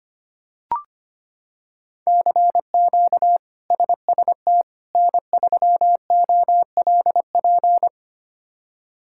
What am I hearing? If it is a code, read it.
Morse code, heard as a keyed tone, about 25 wpm, 700 Hz. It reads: CQ SST N3OLP
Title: K1USN SST QSO Practice - 25wpm